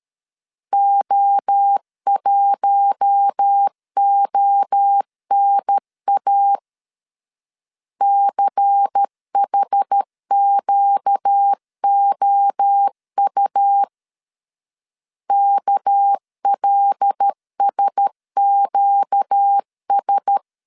Morse code, heard as a keyed tone, about 13 wpm, 800 Hz. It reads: O1ONA CHQOU KLSQS
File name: CW / DET / 1 6 1500C S A 12wpm 3440